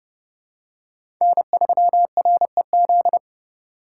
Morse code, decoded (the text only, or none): N3REZ